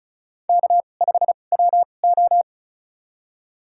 Morse code, heard as a keyed tone, about 36 wpm, spaced as for 25 wpm, 700 Hz. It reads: K5WO
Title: USA Call Signs - 2x Character Spacing - 35wpm